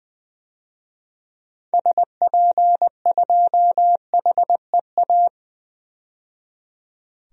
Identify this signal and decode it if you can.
Morse code, heard as a keyed tone, about 20 wpm, 700 Hz. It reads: SP2HEA
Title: DXCC-100 Call Signs - 6 Characters - 20wpm